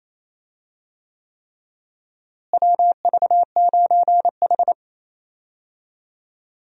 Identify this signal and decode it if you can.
Morse code, heard as a keyed tone, about 28 wpm, 700 Hz. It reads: WV9H
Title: USA Call Signs - 4 Characters - Repeated 3x - 28wpm